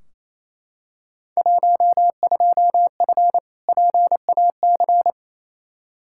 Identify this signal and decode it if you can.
Morse code, heard as a keyed tone, about 28 wpm, 700 Hz. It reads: 12F PAC
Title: Field Day Exchanges Encore - 28wpm